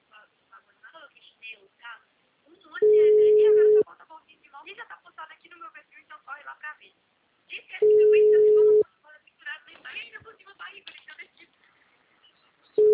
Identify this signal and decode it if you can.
Ringback tone (Europe, 425 Hz), with speech.